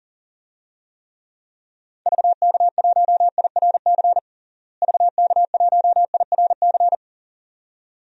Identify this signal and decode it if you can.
Morse code, heard as a keyed tone, about 40 wpm, 700 Hz. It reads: VK1IRC VK1IRC
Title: DXCC-100 Call Signs - 6 Characters - Repeated 2x - 40wpm